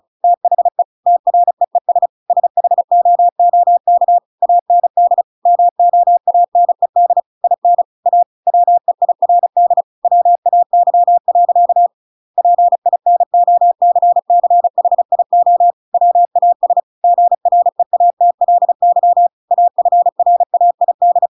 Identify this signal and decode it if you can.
Morse code, heard as a keyed tone, about 35 wpm, 700 Hz. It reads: THE TREES SHOOK AND MOANED IN A WEIRD WAY. PINOCCHIO WAS GREATLY AFRAID
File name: Pinocchio / Pinocchio ch6 35 35 ik1hge